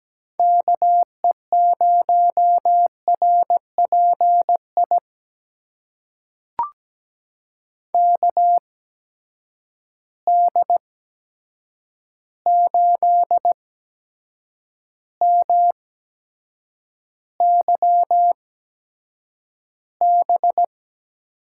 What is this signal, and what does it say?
Morse code, heard as a keyed tone, about 17 wpm, spaced as for 7 wpm, 700 Hz. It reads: KE0RPI KD8MYB
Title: USA Call Signs - 8x Character Spacing - 17wpm